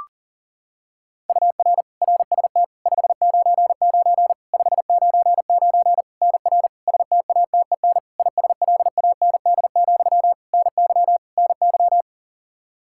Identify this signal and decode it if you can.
Morse code, heard as a keyed tone, about 40 wpm, 700 Hz. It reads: UR RST 599 599 NR STATEN ISLAND, NY NY
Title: Practice QSO - Exchange 1 - Beginning - 40wpm